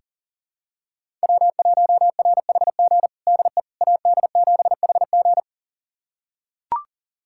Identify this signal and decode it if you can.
Morse code, heard as a keyed tone, about 40 wpm, 700 Hz. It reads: W1RHG DE AD7HG
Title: USA Callsign DE Callsign - 5 Characters - 40wpm